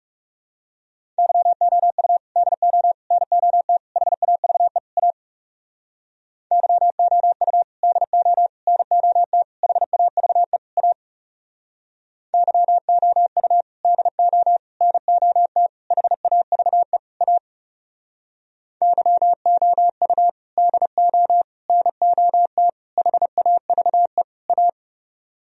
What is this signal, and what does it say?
Morse code, heard as a keyed tone, about 37 wpm, 700 Hz. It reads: YOU DO NOT HAVE A YOU DO NOT HAVE A YOU DO NOT HAVE A YOU DO NOT HAVE A
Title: Speed-Racing - Sets of 5 Words 30wpm